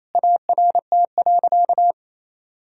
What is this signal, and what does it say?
Morse code, heard as a keyed tone, about 28 wpm, 700 Hz. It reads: ART.
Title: Sentences from Top 1000 Words - Review Part 3 - 28wpm